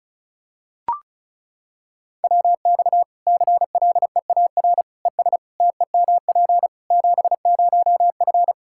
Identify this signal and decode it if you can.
Morse code, heard as a keyed tone, about 36 wpm, 700 Hz. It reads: WX CLEAR ES TEMP 70F
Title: K1USN SST QSO Practice 35wpm